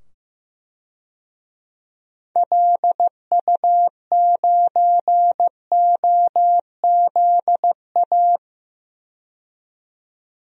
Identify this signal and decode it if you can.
Morse code, heard as a keyed tone, about 15 wpm, 700 Hz. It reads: LU9OZA
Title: DXCC-100 Call Signs - 6 Characters - 15wpm